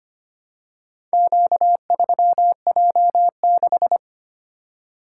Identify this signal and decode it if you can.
Morse code, heard as a keyed tone, about 25 wpm, 700 Hz. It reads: Q3J6
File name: practice / 25wpm / Four Letter-Number Combinations - 25wpm